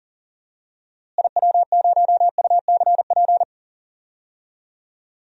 Morse code, heard as a keyed tone, about 38 wpm, 700 Hz. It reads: IW0UCP
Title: DXCC-100 Call Signs - 6 Characters - Repeated 3x - 40wpm